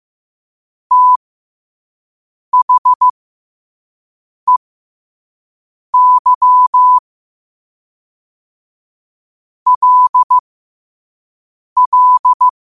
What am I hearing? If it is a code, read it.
Morse code, heard as a keyed tone, about 15 wpm, spaced as for 5 wpm, 1000 Hz. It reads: THEY LL
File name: qotdfiles / 5wpm